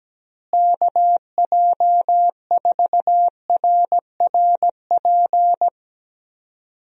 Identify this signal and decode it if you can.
Morse code, heard as a keyed tone, about 17 wpm, 700 Hz. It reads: KJ4RRP
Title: USA Call Signs - 6 Characters - Repeated 2x - 17wpm